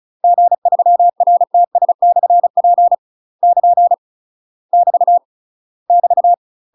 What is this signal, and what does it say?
Morse code, heard as a keyed tone, about 34 wpm, 700 Hz. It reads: G3RTS/P <KN> <BT> <BT>
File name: UK MASTER QSOs TRAINING / 35WPM / UK MASTER QSOs 0025